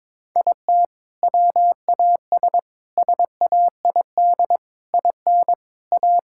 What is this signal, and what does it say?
Morse code, heard as a keyed tone, about 21 wpm, 700 Hz. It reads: IT WAS SAID IN A
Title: Sentences from Top 1600 Words 22wpm